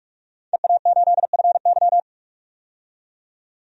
Morse code, heard as a keyed tone, about 42 wpm, 700 Hz. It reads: EA8FY
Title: DXCC-100 Call Signs - 5 Characters - 45wpm